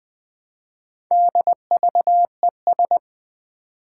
Morse code, heard as a keyed tone, about 20 wpm, 700 Hz. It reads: DVES
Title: Four Challenging Letter-Number Combinations 20wpm